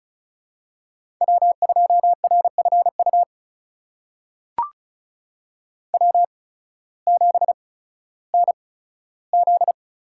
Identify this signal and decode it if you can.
Morse code, heard as a keyed tone, about 36 wpm, spaced as for 16 wpm, 700 Hz. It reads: W2RFU W7NZ
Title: USA Call Signs - 5 Characters - 8x Character Spacing - 35wpm